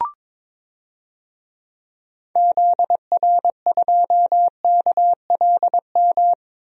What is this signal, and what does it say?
Morse code, heard as a keyed tone, about 22 wpm, 700 Hz. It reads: ZR2KLM
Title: DXCC-100 Call Signs - 6 Characters - Repeated 2x - 22wpm